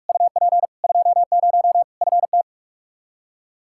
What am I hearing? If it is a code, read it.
Morse code, heard as a keyed tone, about 41 wpm, 700 Hz. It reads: UP 20 FT
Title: Practice QSO - Exchange 2 - Info - 45wpm